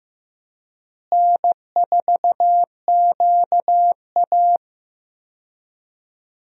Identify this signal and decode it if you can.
Morse code, heard as a keyed tone, about 15 wpm, 700 Hz. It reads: N4QA
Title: USA Call Signs - 4 Characters - Repeated 3x - 15wpm